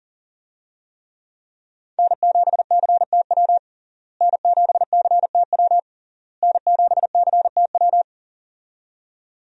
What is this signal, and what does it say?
Morse code, heard as a keyed tone, about 40 wpm, 700 Hz. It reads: N7CTW N7CTW N7CTW
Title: USA Call Signs - 5 Characters - Repeated 3x - 40wpm